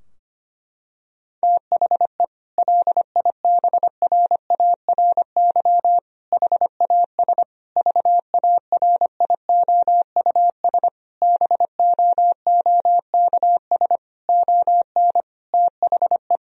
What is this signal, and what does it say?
Morse code, heard as a keyed tone, about 25 wpm, 700 Hz. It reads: THE LIBRARY HAS VARIOUS BOOKS ON THE
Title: Sentences from Top 1000 Words - Review Part 4 - 25wpm